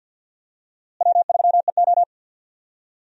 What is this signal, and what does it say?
Morse code, heard as a keyed tone, about 52 wpm, 700 Hz. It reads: W3EO
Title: USA Call Signs - 4 Characters - Repeated 3x - 50wpm